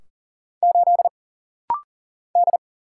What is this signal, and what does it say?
Morse code, heard as a keyed tone, about 40 wpm, 700 Hz. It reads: 8 D
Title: Single Letter-Number - Rapid-Fire 40wpm